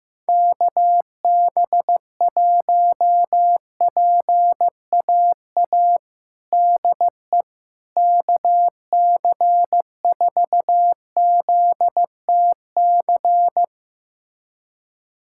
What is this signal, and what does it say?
Morse code, heard as a keyed tone, about 15 wpm, 700 Hz. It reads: KB1PAA DE KC4ZTC
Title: USA Callsign DE Callsign - 6 Characters - 15wpm